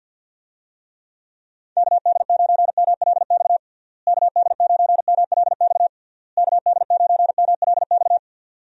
Morse code, heard as a keyed tone, about 49 wpm, 700 Hz. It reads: KD9MLX KD9MLX KD9MLX
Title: USA Call Signs - 6 Characters - Repeated 3x - 50wpm